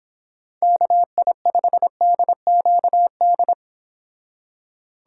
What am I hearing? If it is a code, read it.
Morse code, heard as a keyed tone, about 26 wpm, 700 Hz. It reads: KI5DQD